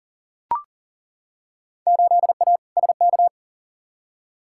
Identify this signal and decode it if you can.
Morse code, heard as a keyed tone, about 40 wpm, 700 Hz. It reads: 8A SK
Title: Field Day Exchanges Encore - 40wpm